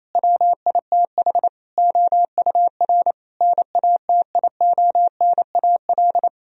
Morse code, heard as a keyed tone, about 28 wpm, 700 Hz. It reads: WITH OUR NATIONAL
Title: Sentences from Top 1000 Words - Review Part 4 - 28wpm